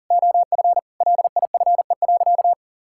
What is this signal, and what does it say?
Morse code, heard as a keyed tone, about 40 wpm, 700 Hz. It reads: OF LIFE.
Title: Sentences from Top 1000 Words - Review Part 6 - 40wpm